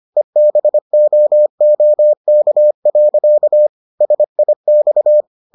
Morse code, heard as a keyed tone, about 25 wpm, 600 Hz. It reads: EBOOK. SIX